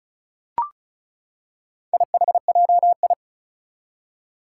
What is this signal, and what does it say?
Morse code, heard as a keyed tone, about 36 wpm, 700 Hz. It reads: IHJI
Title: Four Challenging Letter-Number Combinations 35wpm